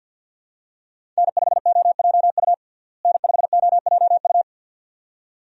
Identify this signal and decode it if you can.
Morse code, heard as a keyed tone, about 48 wpm, 700 Hz. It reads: N5OJU N5OJU
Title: USA Call Signs - 5 Characters - Repeated 2x - 50wpm